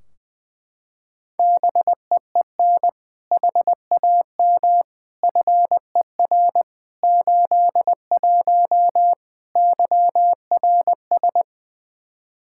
Morse code, heard as a keyed tone, about 20 wpm, 700 Hz. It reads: BEEN HAM FER 81 YRS
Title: Practice QSO - Exchange 3 - More Info - 20wpm